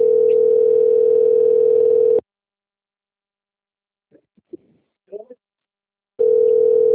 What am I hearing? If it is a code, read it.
Ringback tone (North America), 440 + 480 Hz, with speech.
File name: out- - - - - .29840